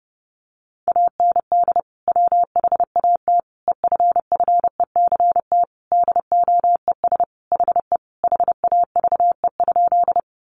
Morse code, heard as a keyed tone, about 30 wpm, 700 Hz. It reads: AND WHAT EFFECT DOES HE HAVE?
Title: Sentences from Top 1000 Words 30wpm